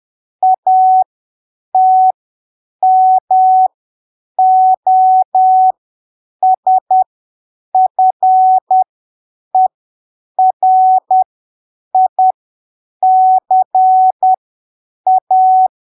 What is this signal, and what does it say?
Morse code, heard as a keyed tone, about 10 wpm, spaced as for 7 wpm, 750 Hz. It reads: ATMOSFERICA
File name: arquivos / texto / texto 10 10ppm